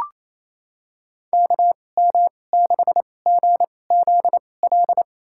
Morse code, heard as a keyed tone, about 28 wpm, spaced as for 20 wpm, 700 Hz. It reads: KM6GZL
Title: USA Call Signs - 2x Character Spacing - 28wpm